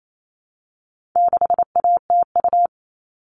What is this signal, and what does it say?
Morse code, heard as a keyed tone, about 28 wpm, 700 Hz. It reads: 6ATU
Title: All Cut Numbers - 4-Digit 28wpm